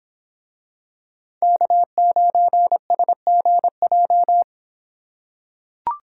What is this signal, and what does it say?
Morse code, heard as a keyed tone, about 25 wpm, 700 Hz. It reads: K9SGJ